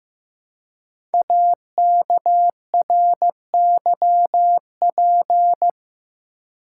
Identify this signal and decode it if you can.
Morse code, heard as a keyed tone, about 15 wpm, 700 Hz. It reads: AKRYP
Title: Code Groups - 5 Letters - 15wpm